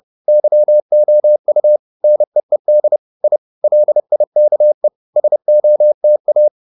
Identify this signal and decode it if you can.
Morse code, heard as a keyed tone, about 30 wpm, 600 Hz. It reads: YOU NEED I LIKE SOTA